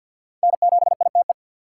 Morse code, heard as a keyed tone, about 50 wpm, 690 Hz. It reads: N7ITE
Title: USA Call Signs - 5 Characters - 3x Character Spacing - 50wpm